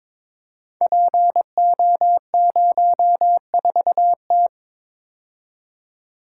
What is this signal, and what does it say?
Morse code, heard as a keyed tone, about 22 wpm, 700 Hz. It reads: PO04T
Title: Code Groups - 5 Letter:Numbers - 22wpm